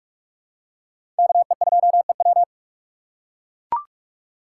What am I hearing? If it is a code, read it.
Morse code, heard as a keyed tone, about 43 wpm, 700 Hz. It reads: KE2EW
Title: USA Call Signs - 2x Character Spacing - 45wpm